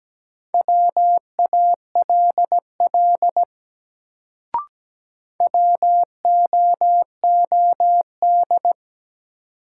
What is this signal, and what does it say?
Morse code, heard as a keyed tone, about 17 wpm, 700 Hz. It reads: WALL WOOD
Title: Sentences from Top 1100 Words - New Word Drill Alt - 17wpm